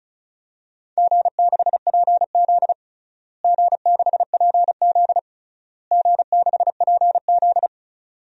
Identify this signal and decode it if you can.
Morse code, heard as a keyed tone, about 36 wpm, 690 Hz. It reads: G6PZ G6PZ G6PZ
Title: DXCC-100 Call Signs - 4 Characters - Repeated 3x - 35wpm